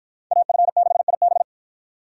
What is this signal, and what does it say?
Morse code, heard as a keyed tone, about 53 wpm, 700 Hz. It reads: AF6IB